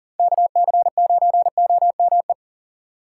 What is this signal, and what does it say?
Morse code, heard as a keyed tone, about 37 wpm, 700 Hz. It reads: KC9OME
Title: USA Call Signs - 6 Characters - 2x Character Spacing - 40wpm